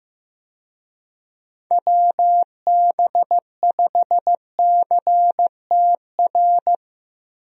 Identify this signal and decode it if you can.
Morse code, heard as a keyed tone, about 15 wpm, 700 Hz. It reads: WB5CTR
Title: USA Call Signs - 6 Characters - Repeated 2x - 15wpm